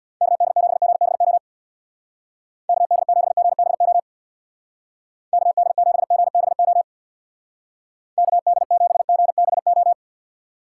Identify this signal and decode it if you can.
Morse code, heard as a keyed tone, about 64 wpm, 700 Hz. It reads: KD7GBO KD7GBO KD7GBO KD7GBO